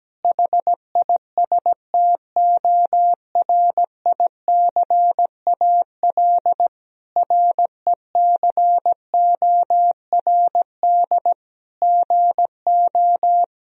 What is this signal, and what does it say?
Morse code, heard as a keyed tone, about 17 wpm, 700 Hz. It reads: HISTORICAL RECORD GO
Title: Sentences from Top 1000 Words - Review Part 4 - 17wpm